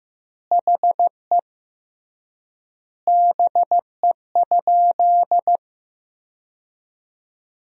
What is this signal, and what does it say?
Morse code, heard as a keyed tone, about 15 wpm, 700 Hz. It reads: HE BE?
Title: 3x Word-Spacing - Sentences from Top 200 Words - 15wpm